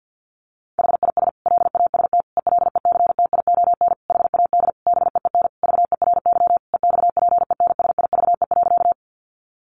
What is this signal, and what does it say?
Morse code, heard as a keyed tone, about 50 wpm, 700 Hz. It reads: HIS LAST ELECTION HAD BEEN VERY EXPENSIVE.
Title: Sentences from Top 1500 Words 50wpm